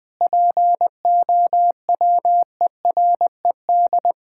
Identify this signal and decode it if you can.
Morse code, heard as a keyed tone, about 20 wpm, 700 Hz. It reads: POWERED
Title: Sentences with 6-Word Phrases - 20wpm